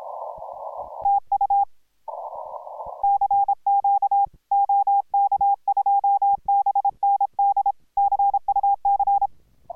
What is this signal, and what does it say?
Morse code, heard as a keyed tone, about 26 wpm, 810 Hz. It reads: TU CQ OK2BND CUC